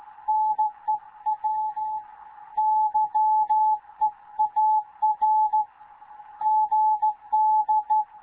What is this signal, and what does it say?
Morse code, heard as a keyed tone, about 14 wpm, 860 Hz. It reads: NEW YEAR GD
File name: Morze-HA2MN / hangp / 2 11 GYAKORLAT